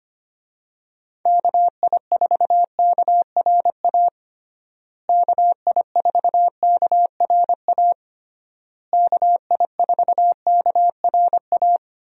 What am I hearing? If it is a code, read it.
Morse code, heard as a keyed tone, about 25 wpm, 700 Hz. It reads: KI4KRA KI4KRA KI4KRA
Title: USA Call Signs - 6 Characters - Repeated 3x - 25wpm